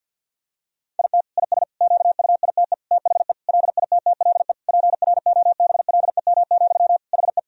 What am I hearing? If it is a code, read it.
Morse code, heard as a keyed tone, about 52 wpm, 700 Hz. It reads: IT IS QUITE THE LITTLE PROBLEM, HE
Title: Sentences from Top 1000 Words - Review Part 2 - 50wpm